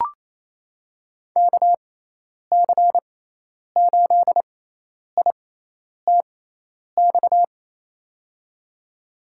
Morse code, heard as a keyed tone, about 27 wpm, spaced as for 10 wpm, 700 Hz. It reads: KC8ITX
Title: USA Call Signs - 6x Character Spacing - 28wpm